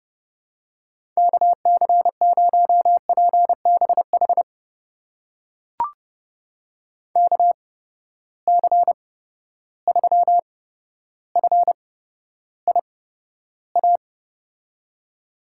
Morse code, heard as a keyed tone, about 31 wpm, spaced as for 13 wpm, 700 Hz. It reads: KC0PBH KC3FIA
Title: USA Call Signs - 6 Characters - 8x Character Spacing - 30wpm